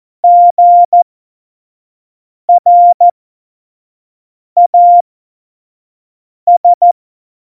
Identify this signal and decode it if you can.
Morse code, heard as a keyed tone, about 14 wpm, spaced as for 5 wpm, 700 Hz. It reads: GRAS
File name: Morsekurs / Witze / Witz 16. Oktober 2018, 25 F70